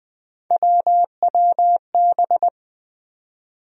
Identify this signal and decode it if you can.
Morse code, heard as a keyed tone, about 20 wpm, 700 Hz. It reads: WWB
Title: USA Call Sign Postfixes - 3 Letters 20wpm